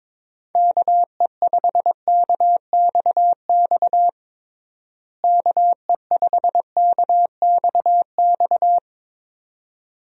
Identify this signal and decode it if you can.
Morse code, heard as a keyed tone, about 21 wpm, 700 Hz. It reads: KE5KXX KE5KXX
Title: USA Call Signs - 6 Characters - Repeated 2x - 22wpm